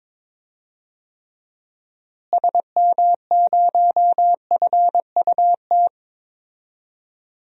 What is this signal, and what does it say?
Morse code, heard as a keyed tone, about 22 wpm, 700 Hz. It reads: SM0FUT
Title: DXCC-100 Call Signs - 6 Characters - Repeated 2x - 22wpm